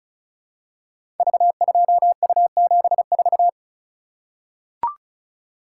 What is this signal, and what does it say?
Morse code, heard as a keyed tone, about 36 wpm, 700 Hz. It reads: V2U74
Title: Code Groups - 5 Letter:Numbers - 35wpm